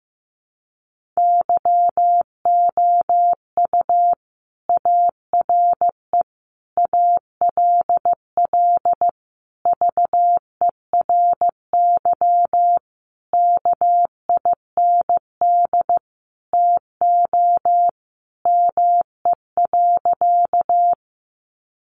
Morse code, heard as a keyed tone, about 15 wpm, 700 Hz. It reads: YOU ARE ALL VERY KIND TO ME.
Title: Sentences from Top 200 Words 15wpm